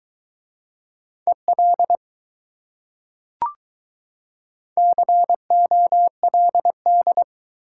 Morse code, heard as a keyed tone, about 22 wpm, 700 Hz. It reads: EL COLD